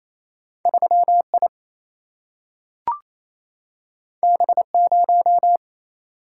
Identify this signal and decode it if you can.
Morse code, heard as a keyed tone, about 28 wpm, 700 Hz. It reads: 3 I B 0
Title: Single-Letter Single-Number 28wpm